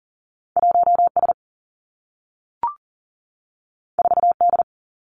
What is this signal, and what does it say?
Morse code, heard as a keyed tone, about 38 wpm, 700 Hz. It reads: 1 S 4 D
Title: Two Challenging Letter-Number Combinations 40wpm